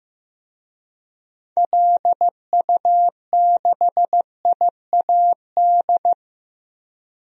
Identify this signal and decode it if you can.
Morse code, heard as a keyed tone, about 16 wpm, 700 Hz. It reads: LU6IAD
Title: DXCC-100 Call Signs - 6 Characters - 15wpm